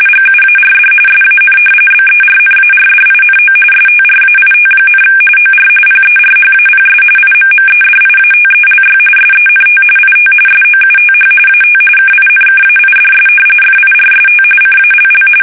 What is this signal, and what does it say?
RTTY, about 45.45 baud. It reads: GUARD SAN BERNARDINO STRAIT, OR PERHAPS IMMEDIATELY SENT IT SOUTH AT KINKAIDS FIRST CALL
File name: fsk850 / battleshipnj